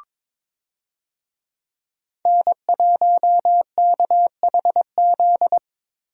Morse code, heard as a keyed tone, about 22 wpm, 700 Hz. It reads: N1KHZ
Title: USA Call Signs - 5 Characters - Repeated 2x - 22wpm